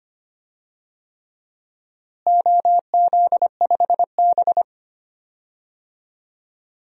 Morse code, heard as a keyed tone, about 25 wpm, 700 Hz. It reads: OZ5B
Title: DXCC-100 Call Signs - 4 Characters - Repeated 3x - 25wpm